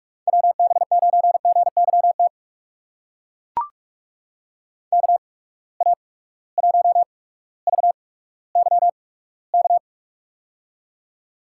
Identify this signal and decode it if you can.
Morse code, heard as a keyed tone, about 44 wpm, spaced as for 19 wpm, 700 Hz. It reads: WB9GYT KA1VYK
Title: USA Call Signs - 6 Characters - 8x Character Spacing - 45wpm